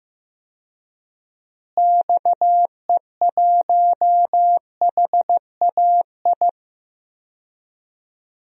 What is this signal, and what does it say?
Morse code, heard as a keyed tone, about 15 wpm, 700 Hz. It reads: XE1HAI
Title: DXCC-100 Call Signs - 6 Characters - Repeated 3x - 15wpm